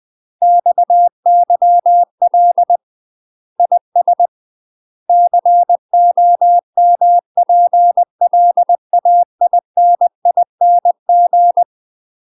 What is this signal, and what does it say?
Morse code, heard as a keyed tone, about 20 wpm, 700 Hz. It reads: XYL IS COMPLAINING
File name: UK MASTER QSOs TRAINING / 20WPM / UK MASTER QSOs 0044